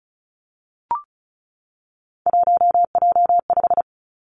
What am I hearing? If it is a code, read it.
Morse code, heard as a keyed tone, about 36 wpm, 700 Hz. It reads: 1J5
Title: Three Challenging Letter-Number Combinations 35wpm